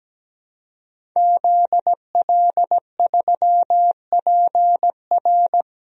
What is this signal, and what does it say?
Morse code, heard as a keyed tone, about 17 wpm, 700 Hz. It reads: ZL3PR
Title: DXCC-100 Call Signs - 5 Characters - 17wpm